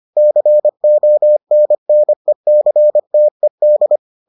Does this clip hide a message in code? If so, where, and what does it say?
Yes: Morse code from 0.2 to 4.0 s, heard as a keyed tone, about 25 wpm, 600 Hz. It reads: CONNECTED